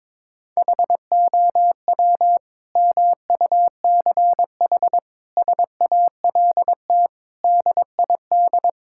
Morse code, heard as a keyed tone, about 22 wpm, 700 Hz. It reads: HOW MUCH SALT DID
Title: Sentences from Top 1000 Words - Review Part 5 - 22wpm